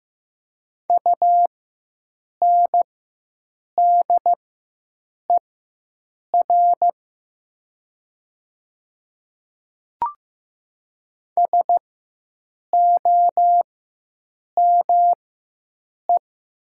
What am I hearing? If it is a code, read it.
Morse code, heard as a keyed tone, about 15 wpm, spaced as for 7 wpm, 700 Hz. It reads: UNDER SOME
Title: 4x Character Spacing - Top 300 Words - 15wpm